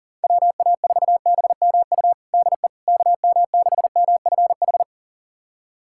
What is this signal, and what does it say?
Morse code, heard as a keyed tone, about 40 wpm, 700 Hz. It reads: WA4BMU DE KM6MFH
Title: USA Callsign DE Callsign 40wpm